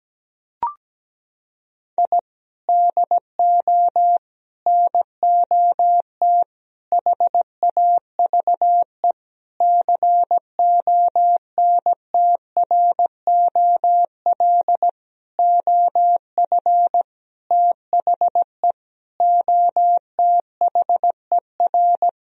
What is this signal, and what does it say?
Morse code, heard as a keyed tone, about 17 wpm, 700 Hz. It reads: I DO NOT HAVE CONTROL OF THE OTHER
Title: Sentences from Top 1000 Words - Review Part 3 - 17wpm